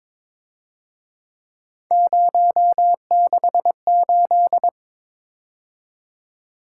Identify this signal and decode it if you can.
Morse code, heard as a keyed tone, about 26 wpm, 700 Hz. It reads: 068
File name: Speed-Racing - Three Letter-Number Combinations - 22wpm